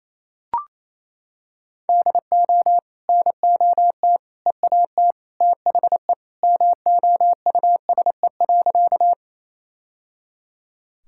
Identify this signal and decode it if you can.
Morse code, heard as a keyed tone, about 28 wpm, 700 Hz. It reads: DO NOT EAT THE MOUSE.
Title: Sentences from Top 1000 Words - Review Part 5 - 28wpm